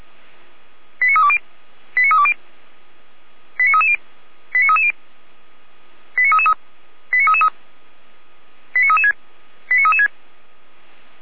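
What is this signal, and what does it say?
Five-tone selective call (ZVEI), heard as quick runs of five tones, tones of about 70 ms. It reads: 89320 89320 89330 89330 89333 89333 89336 89336
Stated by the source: ZVEI five-tone selective call received via SDR, from the Feuerwehr Orlamünde audio server.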